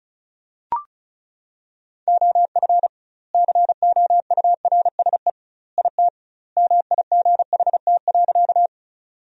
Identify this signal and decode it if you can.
Morse code, heard as a keyed tone, about 34 wpm, 700 Hz. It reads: OF COURSE IT MIGHT.
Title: 2x Word-Spacing - Sentences from Top 500 Words - 35wpm